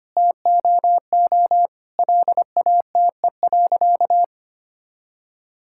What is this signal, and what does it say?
Morse code, heard as a keyed tone, about 25 wpm, 700 Hz. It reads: TOO LATE.
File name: Sentences with 3-Word Phrases - 25wpm